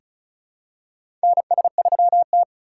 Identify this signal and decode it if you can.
Morse code, heard as a keyed tone, about 36 wpm, 700 Hz. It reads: NS3T
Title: USA Call Signs - 4 Characters - 3x Character Spacing - 35wpm